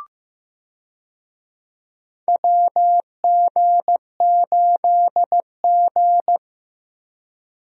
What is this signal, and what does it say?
Morse code, heard as a keyed tone, about 16 wpm, 700 Hz. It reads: WG8G
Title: USA Call Signs - 4 Characters - Repeated 2x - 15wpm